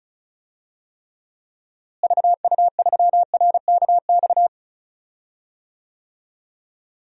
Morse code, heard as a keyed tone, about 34 wpm, 700 Hz. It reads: VU3RKX
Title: DXCC-100 Call Signs - 6 Characters - Repeated 3x - 35wpm